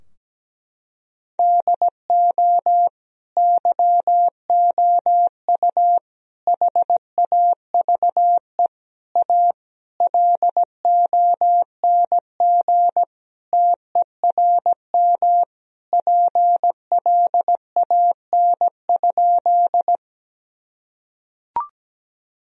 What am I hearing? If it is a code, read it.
Morse code, heard as a keyed tone, about 17 wpm, 700 Hz. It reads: DO YOU HAVE A LONG TERM PLAN?
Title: Sentences from Top 1000 Words - Review Part 4 - 17wpm